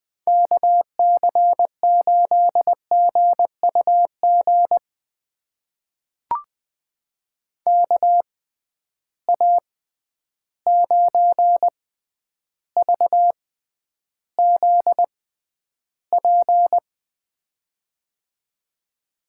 Morse code, heard as a keyed tone, about 20 wpm, spaced as for 10 wpm, 700 Hz. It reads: KC8GUG KA9VZP
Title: USA Call Signs - 6 Characters - 6x Character Spacing - 20wpm